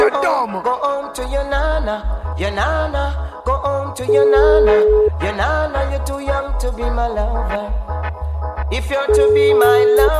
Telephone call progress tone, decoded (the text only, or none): ringback tone